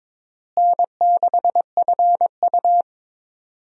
Morse code, heard as a keyed tone, about 22 wpm, 700 Hz. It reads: N6FU
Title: USA Call Signs - 4 Characters - Repeated 3x - 22wpm